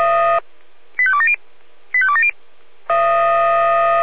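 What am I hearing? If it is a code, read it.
Five-tone selective call (ZVEI), heard as quick runs of five tones, tones of about 70 ms. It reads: 85270 85270
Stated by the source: ZVEI five-tone selective call received via SDR.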